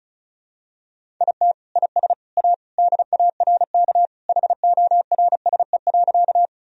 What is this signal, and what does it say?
Morse code, heard as a keyed tone, about 36 wpm, 700 Hz. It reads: IT IS A DARK HORSE.
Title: Sentences from Top 1000 Words - Review Part 1 - 35wpm